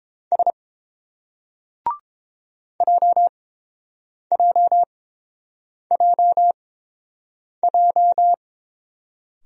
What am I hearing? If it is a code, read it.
Morse code, heard as a keyed tone, about 29 wpm, spaced as for 7 wpm, 700 Hz. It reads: S JJJJ